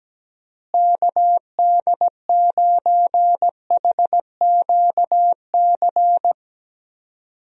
Morse code, heard as a keyed tone, about 17 wpm, 700 Hz. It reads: KD9HQC
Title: USA Call Signs - 6 Characters - Repeated 3x - 17wpm